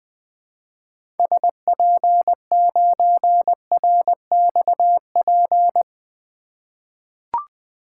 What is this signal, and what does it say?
Morse code, heard as a keyed tone, about 20 wpm, 700 Hz. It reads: SP9RXP
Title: 2023 POTA Hunter Call Signs - 20wpm